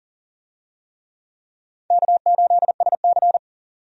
Morse code, heard as a keyed tone, about 40 wpm, 700 Hz. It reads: K8SC
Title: USA Call Signs - 4 Characters - 40wpm